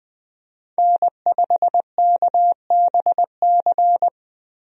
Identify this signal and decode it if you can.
Morse code, heard as a keyed tone, about 21 wpm, 700 Hz. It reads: N5KBC
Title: USA Call Signs - 5 Characters - Repeated 3x - 20wpm